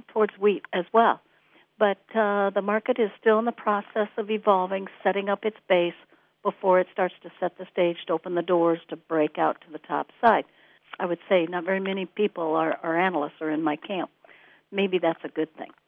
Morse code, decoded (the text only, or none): none